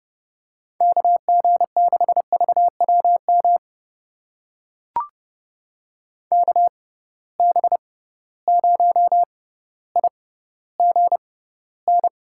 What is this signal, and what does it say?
Morse code, heard as a keyed tone, about 30 wpm, spaced as for 15 wpm, 700 Hz. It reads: KG6VWM KB0IGN
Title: USA Call Signs - 6 Characters - 6x Character Spacing - 30wpm